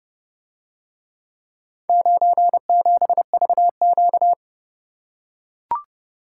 Morse code, heard as a keyed tone, about 30 wpm, 700 Hz. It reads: 97VQ